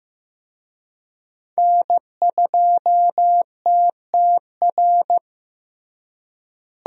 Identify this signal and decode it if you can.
Morse code, heard as a keyed tone, about 15 wpm, 700 Hz. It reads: N2TTR
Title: USA Call Signs - 5 Characters - Repeated 2x - 15wpm